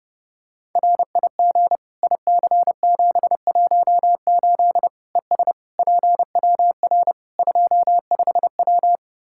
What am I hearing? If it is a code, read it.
Morse code, heard as a keyed tone, about 30 wpm, 700 Hz. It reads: RIG IC718 ES PWR 25W
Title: Practice QSO - Exchange 2 - Info - 30wpm